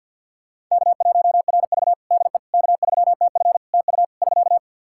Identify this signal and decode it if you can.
Morse code, heard as a keyed tone, about 52 wpm, 700 Hz. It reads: K1RV DE K3TF TU <SK>
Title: Practice QSO - Exchange 4 - Ending - 50wpm